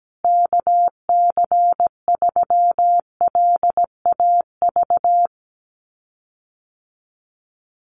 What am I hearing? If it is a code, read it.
Morse code, heard as a keyed tone, about 17 wpm, 700 Hz. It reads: KC3LAV
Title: USA Call Signs Repeated 3x 17wpm